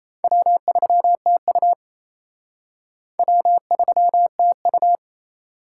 Morse code, heard as a keyed tone, about 33 wpm, 700 Hz. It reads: W3TU W3TU
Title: Speed Racing - USA 1x2 Call Signs 28wpm